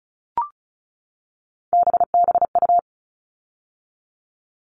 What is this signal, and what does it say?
Morse code, heard as a keyed tone, about 34 wpm, 700 Hz. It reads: BBU
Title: All Cut Numbers - 3-Digit 35wpm